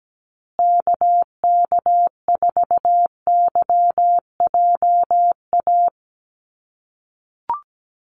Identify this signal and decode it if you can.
Morse code, heard as a keyed tone, about 17 wpm, 700 Hz. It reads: KK4YJA
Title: USA Call Signs Repeated 2x 17wpm